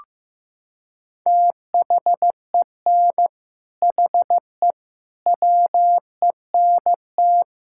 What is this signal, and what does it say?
Morse code, heard as a keyed tone, about 15 wpm, 700 Hz. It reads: THEN HE WENT
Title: Sentences from Top 1000 Words - Review Part 5 - 15wpm